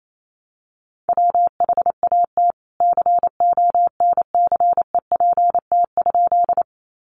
Morse code, heard as a keyed tone, about 28 wpm, 700 Hz. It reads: WHAT CONCEPT?
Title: Sentences from Top 1100 Words 28wpm